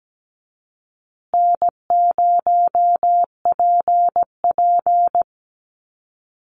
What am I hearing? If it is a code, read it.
Morse code, heard as a keyed tone, about 17 wpm, 700 Hz. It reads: N0PP
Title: Call Signs Repeated 2x - 17wpm